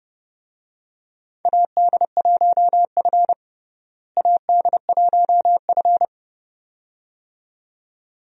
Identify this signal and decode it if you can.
Morse code, heard as a keyed tone, about 32 wpm, 700 Hz. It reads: AD1F AD1F
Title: USA Call Signs - 4 Characters - Repeated 2x - 30wpm